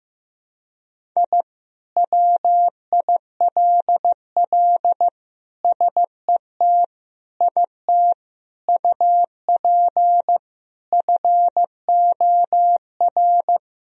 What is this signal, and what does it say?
Morse code, heard as a keyed tone, about 15 wpm, 700 Hz. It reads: I WILL SET IT UP FOR
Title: Sentences from Top 1000 Words - Review Part 1 - 15wpm